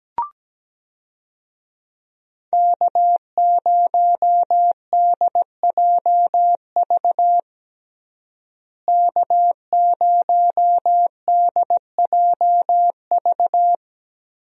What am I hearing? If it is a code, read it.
Morse code, heard as a keyed tone, about 17 wpm, 700 Hz. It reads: K0DJV K0DJV
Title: USA Call Signs - 5 Characters - Repeated 2x - 17wpm